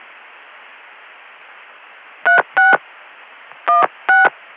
DTMF, heard as telephone touch tones, tones of about 150 ms. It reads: 36 16